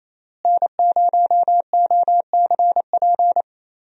Morse code, heard as a keyed tone, about 28 wpm, 700 Hz. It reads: N0OCP
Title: USA Call Signs - 5 Characters - Repeated 3x - 28wpm